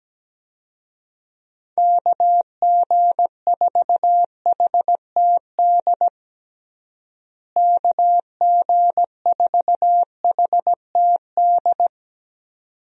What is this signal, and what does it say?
Morse code, heard as a keyed tone, about 17 wpm, 700 Hz. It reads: KG4HTD KG4HTD
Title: USA Call Signs - 6 Characters - Repeated 2x - 17wpm